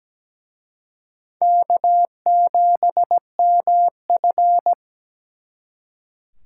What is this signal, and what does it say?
Morse code, heard as a keyed tone, about 20 wpm, 700 Hz. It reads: K7MF
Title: Speed Racing - USA 1x2 Call Signs 17wpm